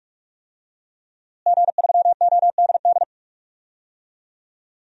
Morse code, heard as a keyed tone, about 45 wpm, 700 Hz. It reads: G3ODD
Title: DXCC-100 Call Signs - 5 Characters - Repeated 3x - 45wpm